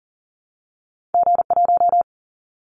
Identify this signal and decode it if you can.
Morse code, heard as a keyed tone, about 40 wpm, 700 Hz. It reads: G1
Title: Callsign Prefixes for Top 25 DXCC - Repeated 3x - 40wpm